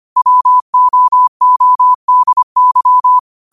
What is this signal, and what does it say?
Morse code, heard as a keyed tone, about 25 wpm, 1000 Hz. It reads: WOODY
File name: qotdfiles / 25wpm